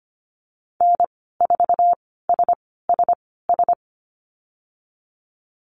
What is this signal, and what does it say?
Morse code, heard as a keyed tone, about 25 wpm, spaced as for 16 wpm, 700 Hz. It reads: N4SSS
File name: Lesson 15g - Number 5 - Call Signs 25wpm